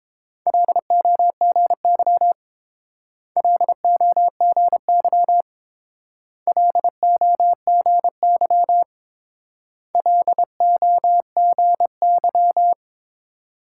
Morse code, heard as a keyed tone, about 28 wpm, 700 Hz. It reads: LOGY LOGY LOGY LOGY